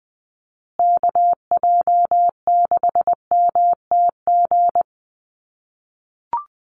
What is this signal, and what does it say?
Morse code, heard as a keyed tone, about 20 wpm, 700 Hz. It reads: KJ6MTG
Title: USA Call Signs Repeated 3x 20wpm